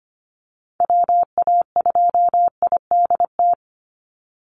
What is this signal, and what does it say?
Morse code, heard as a keyed tone, about 25 wpm, 700 Hz. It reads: WA2IDT
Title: USA Call Signs 25wpm